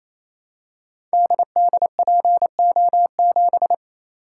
Morse code, heard as a keyed tone, about 28 wpm, 700 Hz. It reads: DDPO7
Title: Code Groups - 5 Letter:Numbers - 28wpm